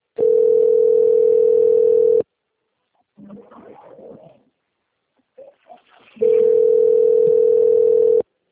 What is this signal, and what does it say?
Ringback tone (North America), 440 + 480 Hz.
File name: rg-1600-unknown- - - .29725